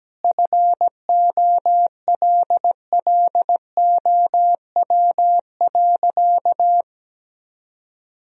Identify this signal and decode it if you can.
Morse code, heard as a keyed tone, about 17 wpm, 700 Hz. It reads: FOLLOW.
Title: Sentences from Top 1000 Words - Review Part 3 - 17wpm